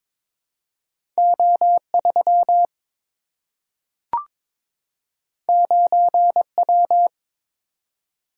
Morse code, heard as a keyed tone, about 23 wpm, 700 Hz. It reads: O3 9W
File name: practice / 22wpm / Two Letter-Number Combinations - 22wpm